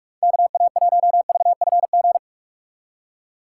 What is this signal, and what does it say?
Morse code, heard as a keyed tone, about 44 wpm, 700 Hz. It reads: KA1VFG
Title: USA Call Signs - 6x Character Spacing - 45wpm